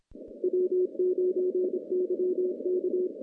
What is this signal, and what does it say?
Morse code, heard as a keyed tone, about 27 wpm, 360 Hz. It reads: W9YK
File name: contests / CW-SS-k5tr- -0000